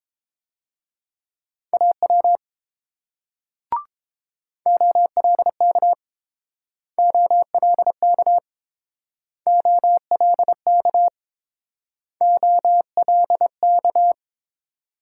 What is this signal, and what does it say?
Morse code, heard as a keyed tone, about 29 wpm, 690 Hz. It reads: AW OLK OLK OLK OLK